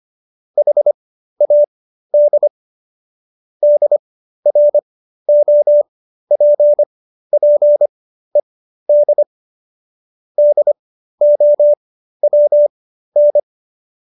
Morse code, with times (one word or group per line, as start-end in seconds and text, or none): 0.6-2.5 HAD
3.6-9.2 DROPPED
10.4-13.4 DOWN